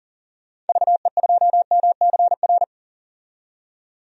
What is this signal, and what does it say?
Morse code, heard as a keyed tone, about 39 wpm, 700 Hz. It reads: VE2MCR
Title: DXCC-100 Call Signs - 6 Characters - 40wpm